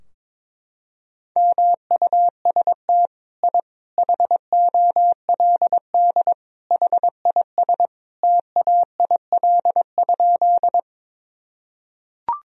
Morse code, heard as a keyed tone, about 22 wpm, 700 Hz. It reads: MUST I HOLD HIS TAIL?
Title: Sentences from Top 1000 Words - Review Part 3 - 22wpm